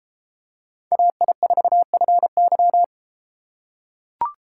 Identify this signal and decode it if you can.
Morse code, heard as a keyed tone, about 31 wpm, 700 Hz. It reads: AI4FY